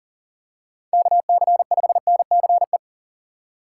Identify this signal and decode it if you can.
Morse code, heard as a keyed tone, about 40 wpm, 700 Hz. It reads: KC5NCE
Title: USA Call Signs - 6 Characters - 8x Character Spacing - 40wpm